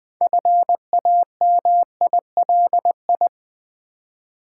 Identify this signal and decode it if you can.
Morse code, heard as a keyed tone, about 20 wpm, 700 Hz. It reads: FAMILI
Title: Build A Word - 3 Letters At A Time - 8-Letter Words - 20wpm